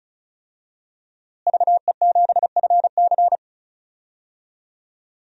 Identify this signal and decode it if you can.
Morse code, heard as a keyed tone, about 36 wpm, 700 Hz. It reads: VE7FC
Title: DXCC-100 Call Signs - 5 Characters - Repeated 2x - 35wpm